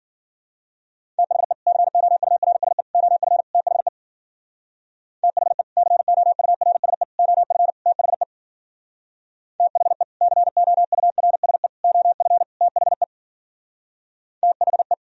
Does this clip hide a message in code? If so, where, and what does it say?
Yes: Morse code from 1.2 to 14.9 s, heard as a keyed tone, about 51 wpm, 700 Hz. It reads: THE COURSE OF THE THE COURSE OF THE THE COURSE OF THE THE